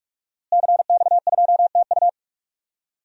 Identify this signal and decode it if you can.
Morse code, heard as a keyed tone, about 41 wpm, 700 Hz. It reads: CX2TU
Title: DXCC-100 Call Signs - 5 Characters - Repeated 2x - 45wpm